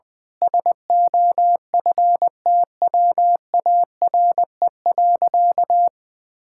Morse code, heard as a keyed tone, about 20 wpm, 700 Hz. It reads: SOFTWARE.
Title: Sentences with 6-Word Phrases - 20wpm